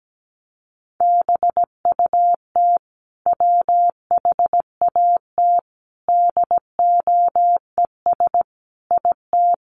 Morse code, heard as a keyed tone, about 17 wpm, 700 Hz. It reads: BUT WHAT DOES IT
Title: Sentences from Top 200 Words 17wpm